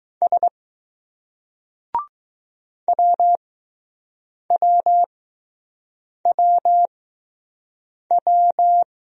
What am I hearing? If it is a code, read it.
Morse code, heard as a keyed tone, about 20 wpm, spaced as for 6 wpm, 700 Hz. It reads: S WWWW